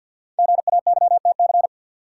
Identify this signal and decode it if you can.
Morse code, heard as a keyed tone, about 52 wpm, 700 Hz. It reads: GAYTC
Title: Code Groups - 5 Letters - 50wpm